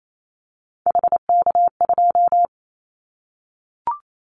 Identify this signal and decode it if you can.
Morse code, heard as a keyed tone, about 28 wpm, 700 Hz. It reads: HK2
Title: Callsign Prefixes for Top 100 DXCC - 28wpm